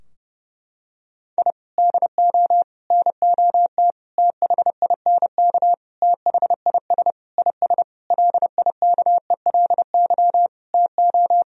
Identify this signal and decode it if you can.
Morse code, heard as a keyed tone, about 30 wpm, 700 Hz. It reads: I DO NOT THINK THIS IS LIKELY TO
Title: Sentences from Top 1000 Words - Review Part 4 - 30wpm